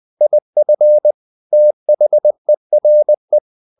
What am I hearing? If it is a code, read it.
Morse code, heard as a keyed tone, about 20 wpm, 600 Hz. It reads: IF THERE
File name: the old man and the sea part 2 20wpm